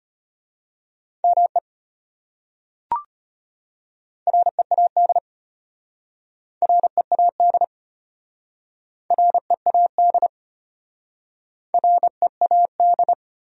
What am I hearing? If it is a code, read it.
Morse code, heard as a keyed tone, about 33 wpm, 700 Hz. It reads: ME READ READ READ READ